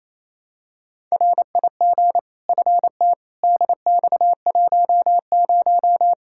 Morse code, heard as a keyed tone, about 28 wpm, 700 Hz. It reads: RIG FT DX10
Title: Practice QSO - Exchange 2 - Info - 28wpm